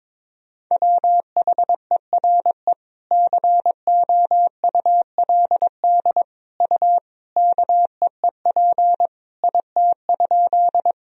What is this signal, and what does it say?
Morse code, heard as a keyed tone, about 21 wpm, 700 Hz. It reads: WHERE COULD U KEEP IT?
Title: Sentences from Top 300 Words and Abbreviations 22wpm